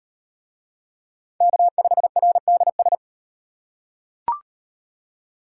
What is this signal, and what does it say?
Morse code, heard as a keyed tone, about 38 wpm, 700 Hz. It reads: K5RDS